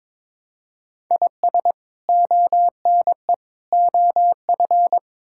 Morse code, heard as a keyed tone, about 22 wpm, 700 Hz. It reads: IS ONE OF
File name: practice / 22wpm / Build A Phrase - 5-Word Phrases - 22wpm